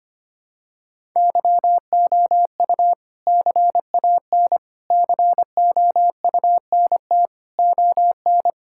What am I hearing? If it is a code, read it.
Morse code, heard as a keyed tone, about 25 wpm, 700 Hz. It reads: YOU CAN COUNT ON
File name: practice / 25wpm / All-Age Appropriate Jokes - 25wpm